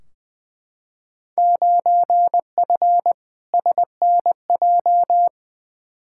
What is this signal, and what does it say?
Morse code, heard as a keyed tone, about 20 wpm, 700 Hz. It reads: 9F SNJ
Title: Field Day Exchanges Encore - 20wpm